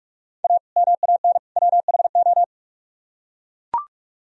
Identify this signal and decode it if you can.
Morse code, heard as a keyed tone, about 44 wpm, 700 Hz. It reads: A MAN WHO